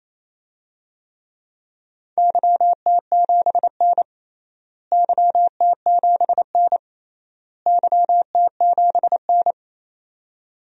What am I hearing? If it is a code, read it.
Morse code, heard as a keyed tone, about 28 wpm, 700 Hz. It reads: YT7N YT7N YT7N
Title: DXCC-100 Call Signs - 4 Characters - Repeated 3x - 28wpm